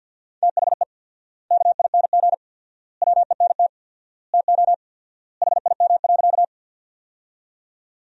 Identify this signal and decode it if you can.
Morse code, heard as a keyed tone, about 51 wpm, 700 Hz. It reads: THE KING WENT TO HIM.
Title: 4x Word-Spacing - Sentences from Top 400 Words - 50wpm